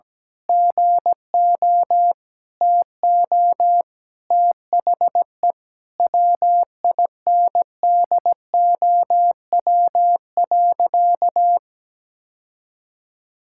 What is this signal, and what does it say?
Morse code, heard as a keyed tone, about 17 wpm, 700 Hz. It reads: GO TO THE WINDOW.
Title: Sentences from Top 1600 Words 17wpm